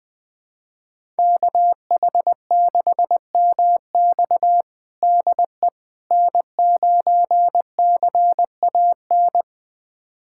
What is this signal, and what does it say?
Morse code, heard as a keyed tone, about 20 wpm, 700 Hz. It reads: KH6MX DE N9CAN
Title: USA Callsign DE Callsign - 5 Characters - 20wpm